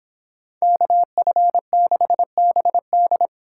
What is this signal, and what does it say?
Morse code, heard as a keyed tone, about 26 wpm, 700 Hz. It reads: KF6BD